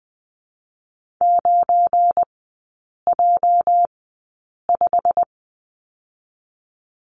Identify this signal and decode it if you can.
Morse code, heard as a keyed tone, about 20 wpm, spaced as for 8 wpm, 700 Hz. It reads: 9J5
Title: 205 Lesson 33e - Letter j - Three Characters 20wpm